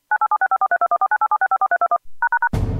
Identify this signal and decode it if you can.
DTMF, heard as telephone touch tones, tones of about 80 ms. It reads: 9876543214987654321#